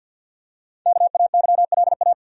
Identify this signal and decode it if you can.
Morse code, heard as a keyed tone, about 51 wpm, 700 Hz. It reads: KAYLA
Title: USA Common Names - 50wpm